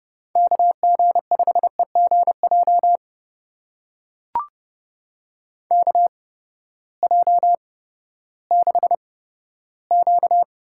Morse code, heard as a keyed tone, about 31 wpm, spaced as for 15 wpm, 700 Hz. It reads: KG5EGJ KJ6Q
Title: USA Call Signs - 6 Characters - 8x Character Spacing - 30wpm